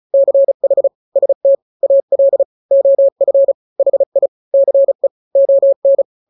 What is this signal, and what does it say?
Morse code, heard as a keyed tone, about 35 wpm, 550 Hz. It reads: CH ST AL OF HI CE ON
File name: n-grams / 2-grams 35 30 WPM